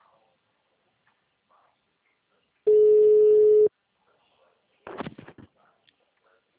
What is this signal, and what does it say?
Ringback tone (Europe, 425 Hz).
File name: out- -2010- - - .6703